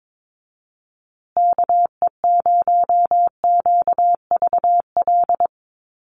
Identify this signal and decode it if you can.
Morse code, heard as a keyed tone, about 22 wpm, 700 Hz. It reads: KE0QVL
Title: USA Call Signs Repeated 2x 22wpm